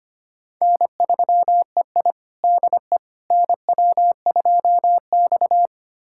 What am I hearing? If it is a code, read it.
Morse code, heard as a keyed tone, about 25 wpm, 700 Hz. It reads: N3EI DE NW2X
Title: USA Callsign DE Callsign - 4 Characters - 25wpm